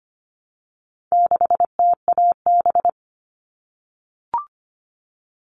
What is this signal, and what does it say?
Morse code, heard as a keyed tone, about 25 wpm, 700 Hz. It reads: 6TAB
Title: All Cut Numbers - 4-Digit 25wpm